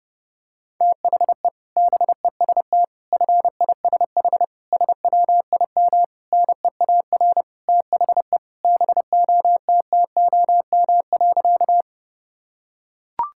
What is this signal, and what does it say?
Morse code, heard as a keyed tone, about 30 wpm, 700 Hz. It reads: THE BEST FISH SWIM NEAR THE BOTTOM.
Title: Sentences from Top 1000 Words - Review Part 6 - 30wpm